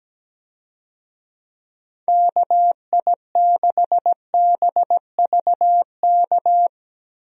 Morse code, heard as a keyed tone, about 17 wpm, 700 Hz. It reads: KI6BVK
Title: USA Call Signs - 6 Characters - Repeated 3x - 17wpm